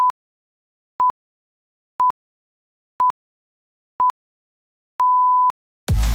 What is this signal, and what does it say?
Radio time signal (time pips). It is heard from 0.0 to 5.5 s.